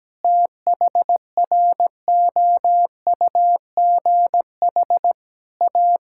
Morse code, heard as a keyed tone, about 17 wpm, 700 Hz. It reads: THROUGH A
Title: Sentences from Top 1000 Words - Review Part 6 - 17wpm